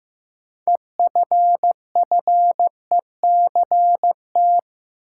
Morse code, heard as a keyed tone, about 15 wpm, 700 Hz. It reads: EFFECT
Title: Sentences from Top 1000 Words - New Word Drill - 15wpm